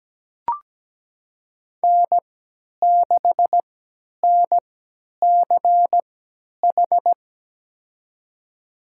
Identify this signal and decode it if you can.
Morse code, heard as a keyed tone, about 17 wpm, spaced as for 10 wpm, 700 Hz. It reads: N6NCH
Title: USA Call Signs - 5 Characters - 3x Character Spacing - 17wpm